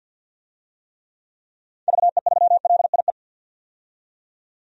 Morse code, heard as a keyed tone, about 47 wpm, 700 Hz. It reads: VE3LIE
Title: DXCC-100 Call Signs - 6 Characters - 50wpm